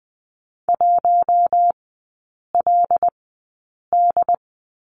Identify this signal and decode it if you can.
Morse code, heard as a keyed tone, about 20 wpm, spaced as for 8 wpm, 700 Hz. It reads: 1LD
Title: 074 Lesson 13e - Letter L - Three Characters 20wpm